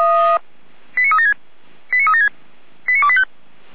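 Five-tone selective call (ZVEI), heard as quick runs of five tones, tones of about 70 ms. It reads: 89376 89376 89285
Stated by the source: Five-tone ZVEI selective call received on SDR.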